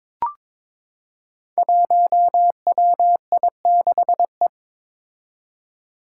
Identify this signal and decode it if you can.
Morse code, heard as a keyed tone, about 22 wpm, 700 Hz. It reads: 1WI6E
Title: Five Challenging Letter-Number Combinations 22wpm